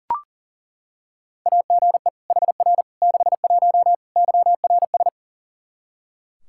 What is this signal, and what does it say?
Morse code, heard as a keyed tone, about 40 wpm, 700 Hz. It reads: AGE HR 61 YRS
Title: Practice QSO - Exchange 3 - More Info - 40wpm